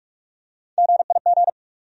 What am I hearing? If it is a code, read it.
Morse code, heard as a keyed tone, about 43 wpm, 700 Hz. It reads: GIG
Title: USA Call Sign Postfixes - 3 Letters 45wpm